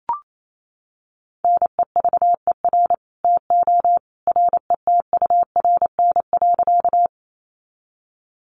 Morse code, heard as a keyed tone, about 28 wpm, 700 Hz. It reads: NEVER TO RETURN.
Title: Sentences from Top 800 Words 28wpm